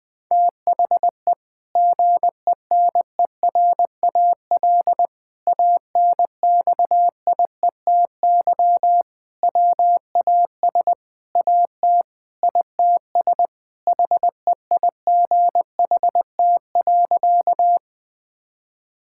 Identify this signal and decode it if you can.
Morse code, heard as a keyed tone, about 20 wpm, 700 Hz. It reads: THE GENERAL ANXIETY WAS AT ITS HEIGHT.
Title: Sentences from Top 1600 Words 20wpm